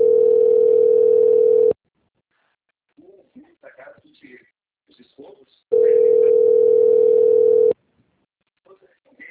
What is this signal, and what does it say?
Ringback tone (North America), 440 + 480 Hz, with speech.
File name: rg-1600-unknown- - - .51582